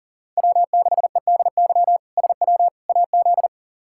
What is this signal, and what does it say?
Morse code, heard as a keyed tone, about 37 wpm, 700 Hz. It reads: W6EDY SW AZ
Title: K1USN SST QSO Practice - 40wpm